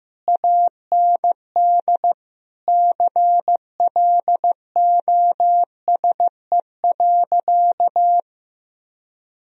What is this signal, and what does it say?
Morse code, heard as a keyed tone, about 15 wpm, 700 Hz. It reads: AND CLOSE.
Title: Sentences from Top 1000 Words - Review Part 1 - 15wpm